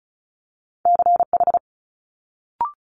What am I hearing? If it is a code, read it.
Morse code, heard as a keyed tone, about 36 wpm, 700 Hz. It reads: CH
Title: Multi-letter Phonemes Quick Fire 35wpm